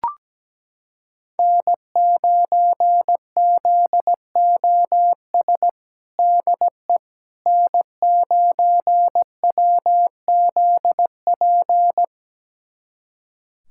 Morse code, heard as a keyed tone, about 17 wpm, 700 Hz. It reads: N9ZOS DE N9WZP
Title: USA Callsign DE Callsign - 5 Characters - 17wpm